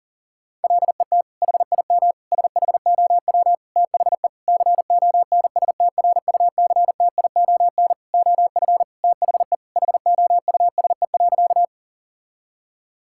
Morse code, heard as a keyed tone, about 40 wpm, 700 Hz. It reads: LET HIM SHOW THE CONSTRUCTION OF THE HOUSE.
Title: Sentences from Top 1600 Words 40wpm